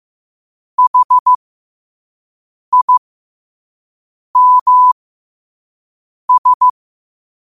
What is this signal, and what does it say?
Morse code, heard as a keyed tone, about 15 wpm, spaced as for 5 wpm, 1000 Hz. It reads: HIMS
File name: qotdfiles / 5wpm